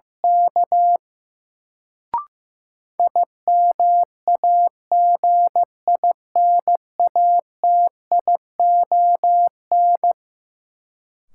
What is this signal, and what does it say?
Morse code, heard as a keyed tone, about 15 wpm, 700 Hz. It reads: K IMAGINATION
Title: Sentences from Top 1300 Words - New Word Drill Alt - 15wpm